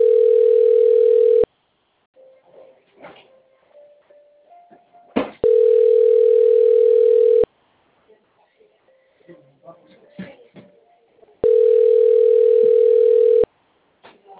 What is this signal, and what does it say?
Ringback tone (North America), 440 + 480 Hz, with speech.
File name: rg-1600-unknown- - - .76874